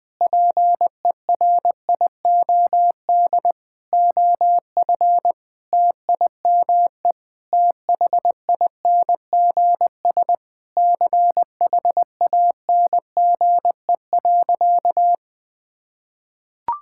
Morse code, heard as a keyed tone, about 20 wpm, 700 Hz. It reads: PERIOD OF TIME THINGS CHANGE.
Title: Sentences with 6-Word Phrases - 20wpm